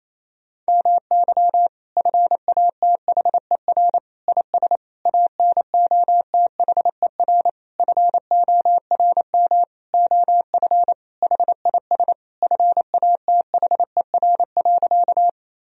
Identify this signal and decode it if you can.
Morse code, heard as a keyed tone, about 28 wpm, 700 Hz. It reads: MY FATHER IS ANOTHER FORM OF HIS FATHER.
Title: Sentences from Top 1000 Words - Review Part 1 - 28wpm